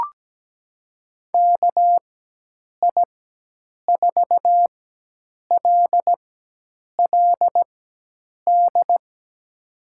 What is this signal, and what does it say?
Morse code, heard as a keyed tone, about 17 wpm, spaced as for 8 wpm, 700 Hz. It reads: KI4LLD
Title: USA Call Signs - 4x Character Spacing - 17wpm